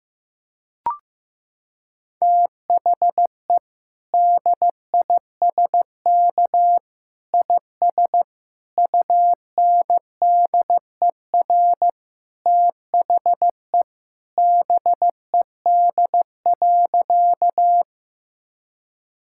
Morse code, heard as a keyed tone, about 15 wpm, 700 Hz. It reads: THE DISK IS UNDER THE BED.
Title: Sentences from Top 1000 Words - Review Part 6 - 15wpm